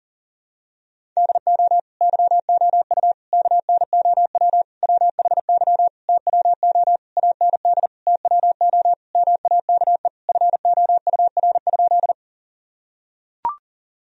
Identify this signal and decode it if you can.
Morse code, heard as a keyed tone, about 40 wpm, 700 Hz. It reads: DO YOU KNOW WHY TWO AND TWO MAKE FOUR?
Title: Sentences from Top 1000 Words - Review Part 1 - 40wpm